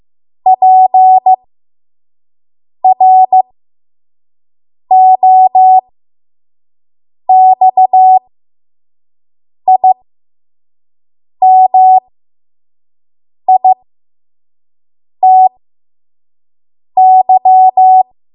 Morse code, heard as a keyed tone, about 15 wpm, spaced as for 5 wpm, 750 Hz. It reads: PROXIMITY
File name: Morse / 05WPM